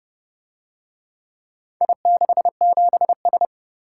Morse code, heard as a keyed tone, about 29 wpm, 700 Hz. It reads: I67S